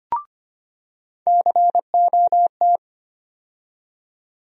Morse code, heard as a keyed tone, about 25 wpm, 700 Hz. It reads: COT
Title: USA Call Sign Postfixes - 3 Letters 25wpm